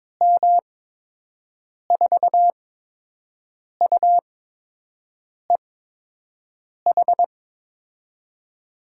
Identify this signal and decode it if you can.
Morse code, heard as a keyed tone, about 22 wpm, spaced as for 6 wpm, 700 Hz. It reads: M4UEH
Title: USA Call Signs - 6 Characters - 8x Character Spacing - 22wpm